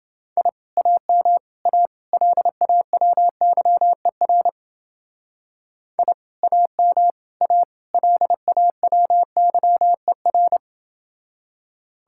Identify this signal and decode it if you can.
Morse code, heard as a keyed tone, about 29 wpm, 700 Hz. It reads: I AM A LAWYER I AM A LAWYER